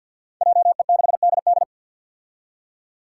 Morse code, heard as a keyed tone, about 53 wpm, 700 Hz. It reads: JE6DD
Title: Five Challenging Letter-Number Combinations 50wpm